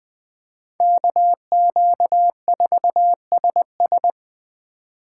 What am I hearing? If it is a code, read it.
Morse code, heard as a keyed tone, about 20 wpm, 700 Hz. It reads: KQ4SS
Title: Speed Racing - USA 2x2 Call Signs 17wpm